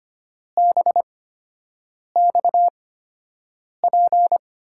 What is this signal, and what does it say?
Morse code, heard as a keyed tone, about 25 wpm, spaced as for 7 wpm, 700 Hz. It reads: BXP
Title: USA Call Signs - 8x Character Spacing - 25wpm